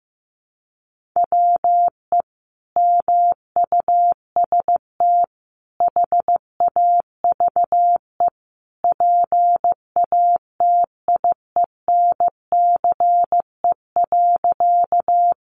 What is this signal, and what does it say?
Morse code, heard as a keyed tone, about 15 wpm, 700 Hz. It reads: WE MUST HAVE PATIENCE.
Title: Sentences from Top 1300 Words 15wpm